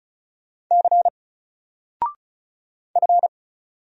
Morse code, heard as a keyed tone, about 36 wpm, 700 Hz. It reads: C F
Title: Single Letter-Number 35wpm